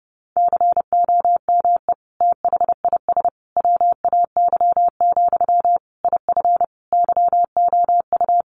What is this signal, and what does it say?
Morse code, heard as a keyed tone, about 28 wpm, 700 Hz. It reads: COME THIS WAY, IF YOU
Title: Sentences from Top 1500 Words 30wpm